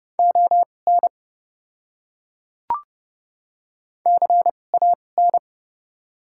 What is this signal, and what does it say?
Morse code, heard as a keyed tone, about 30 wpm, spaced as for 22 wpm, 690 Hz. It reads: ON CAN
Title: 2x Character Spacing - Top 500 Words - 30wpm